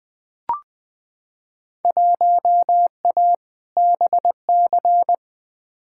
Morse code, heard as a keyed tone, about 20 wpm, 700 Hz. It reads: 1A BC
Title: Field Day Exchanges - 20wpm